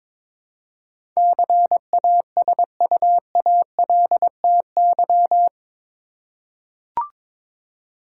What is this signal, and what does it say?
Morse code, heard as a keyed tone, about 22 wpm, 700 Hz. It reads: CASUALTY
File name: practice / 22wpm / Eight Letter Words - 22wpm